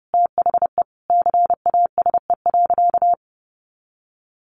Morse code, heard as a keyed tone, about 30 wpm, 700 Hz. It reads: THE CASE.
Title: Sentences from Top 1400 Words 30wpm